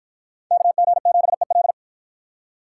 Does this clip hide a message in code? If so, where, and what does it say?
Yes: Morse code from 0.5 to 1.7 s, heard as a keyed tone, about 48 wpm, 700 Hz. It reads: KG7EL